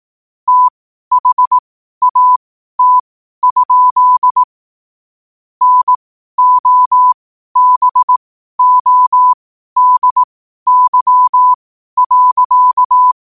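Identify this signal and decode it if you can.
Morse code, heard as a keyed tone, about 18 wpm, spaced as for 12 wpm, 1000 Hz. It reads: THAT? NOBODY.
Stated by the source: Morse code sent at 12 words per minute.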